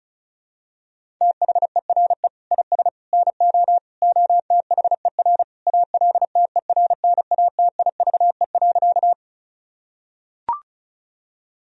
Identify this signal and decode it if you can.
Morse code, heard as a keyed tone, about 34 wpm, 700 Hz. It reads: THERE IS NO OTHER ALTERNATIVE.
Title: Sentences from Top 1000 Words - Review Part 4 - 35wpm